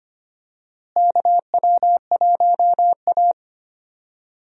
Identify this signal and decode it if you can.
Morse code, heard as a keyed tone, about 25 wpm, 700 Hz. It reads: KW1A
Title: USA Call Signs - 4 Characters - Repeated 2x - 25wpm